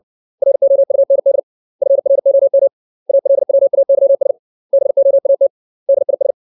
Morse code, heard as a keyed tone, about 60 wpm, 550 Hz. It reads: ROUND FROM ALWAYS BOAT DIS
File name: 632 words / 632 Words 60WPM